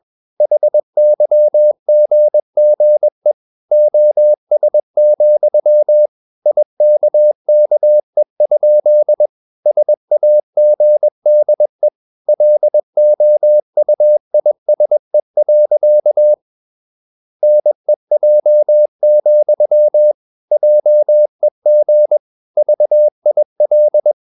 Morse code, heard as a keyed tone, about 21 wpm, 600 Hz. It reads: HYGGE OS, IKKE? SAGDE LOUISE. NEJ, JEG VIL
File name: Ferie QRS 21wpm